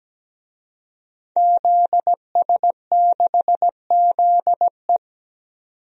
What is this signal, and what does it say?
Morse code, heard as a keyed tone, about 17 wpm, 700 Hz. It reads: ZS6ZE
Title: DXCC-100 Call Signs - 5 Characters - 17wpm